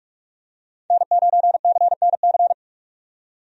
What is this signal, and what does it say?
Morse code, heard as a keyed tone, about 43 wpm, 700 Hz. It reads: N9CNC
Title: USA Call Signs - 5 Characters - 45wpm